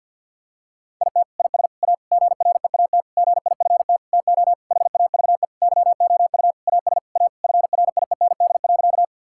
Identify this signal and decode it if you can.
Morse code, heard as a keyed tone, about 49 wpm, 700 Hz. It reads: IT IS A GREAT GIFT TO HAVE YOU AS A FRIEND.
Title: Sentences from Top 1000 Words - Review Part 6 - 50wpm